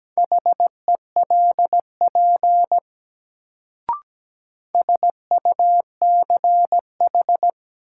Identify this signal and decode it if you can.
Morse code, heard as a keyed tone, about 17 wpm, 700 Hz. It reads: HELP SUCH
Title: Sentences from Top 200 Words - New Word Drill - 17wpm